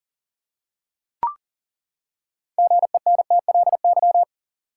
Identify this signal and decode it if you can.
Morse code, heard as a keyed tone, about 40 wpm, 700 Hz. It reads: GENTLY
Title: Six Letter Words - 40wpm